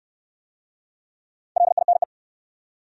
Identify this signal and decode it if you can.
Morse code, heard as a keyed tone, about 66 wpm, 700 Hz. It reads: LINE